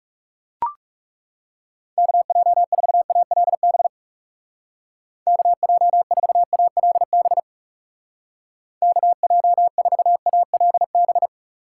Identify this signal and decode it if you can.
Morse code, heard as a keyed tone, about 37 wpm, 700 Hz. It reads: KJ4ALB KJ4ALB KJ4ALB
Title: Speed Racing - USA 2x3 Call Signs 30wpm